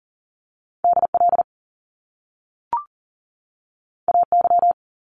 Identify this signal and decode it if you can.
Morse code, heard as a keyed tone, about 40 wpm, 700 Hz. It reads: DL AY
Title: Top Two Letter Combinations in Words 40wpm